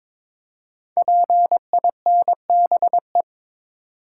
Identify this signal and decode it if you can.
Morse code, heard as a keyed tone, about 22 wpm, 700 Hz. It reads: PINBE
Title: Code Groups - 5 Letter:Numbers - 22wpm